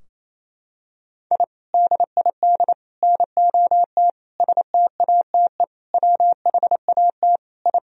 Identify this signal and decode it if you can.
Morse code, heard as a keyed tone, about 28 wpm, 700 Hz. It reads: I DID NOT STATE WHAT I
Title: Sentences from Top 1000 Words - Review Part 2 - 28wpm